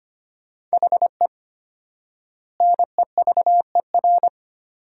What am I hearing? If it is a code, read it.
Morse code, heard as a keyed tone, about 25 wpm, 700 Hz. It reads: HE NEVER
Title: 4x Word-Spacing - Sentences from Top 400 Words - 25wpm